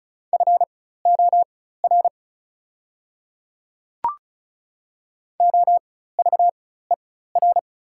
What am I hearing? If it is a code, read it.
Morse code, heard as a keyed tone, about 36 wpm, spaced as for 16 wpm, 700 Hz. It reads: FOR OVER
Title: 4x Character Spacing - Top 100 Words - 35wpm